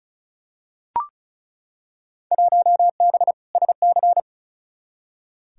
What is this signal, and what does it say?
Morse code, heard as a keyed tone, about 34 wpm, 700 Hz. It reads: 1B SC
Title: Field Day Exchanges - 35wpm